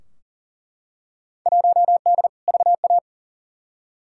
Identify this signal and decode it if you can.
Morse code, heard as a keyed tone, about 40 wpm, 700 Hz. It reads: 1D VA
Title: Field Day Exchanges - 40wpm